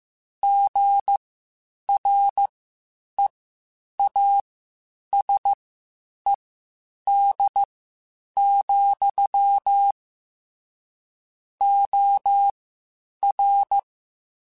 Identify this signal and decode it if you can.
Morse code, heard as a keyed tone, about 15 wpm, spaced as for 8 wpm, 800 Hz. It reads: GREASED, OR
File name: Spaced / Chuckles Spaced-mp3s / Squeaky sp @13